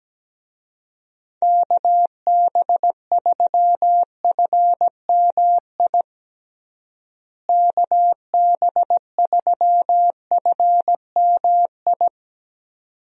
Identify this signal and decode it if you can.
Morse code, heard as a keyed tone, about 17 wpm, 700 Hz. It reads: KB3FMI KB3FMI
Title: USA Call Signs - 6 Characters - Repeated 2x - 17wpm